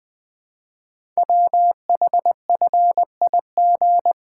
Morse code, heard as a keyed tone, about 20 wpm, 700 Hz. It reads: WHFIG
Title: Code Groups - 5 Letters - 20wpm